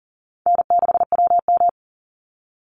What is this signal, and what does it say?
Morse code, heard as a keyed tone, about 40 wpm, 700 Hz. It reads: N6WM
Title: Easy Call Signs Repeated 3x - 40wpm